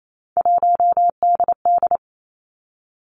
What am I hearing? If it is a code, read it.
Morse code, heard as a keyed tone, about 28 wpm, 700 Hz. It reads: 1DD
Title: Three Challenging Letter-Number Combinations 28wpm